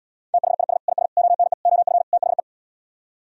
Morse code, heard as a keyed tone, about 77 wpm, 700 Hz. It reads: THIS IS ONE OF THE